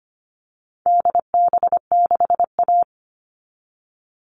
Morse code, heard as a keyed tone, about 25 wpm, 700 Hz. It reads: DB6A
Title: All Cut Numbers - 4-Digit 25wpm